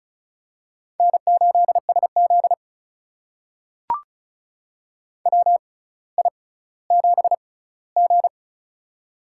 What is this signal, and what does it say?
Morse code, heard as a keyed tone, about 36 wpm, spaced as for 18 wpm, 700 Hz. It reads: N8SZ WI7G
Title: USA Call Signs - 4 Characters - 6x Character Spacing - 35wpm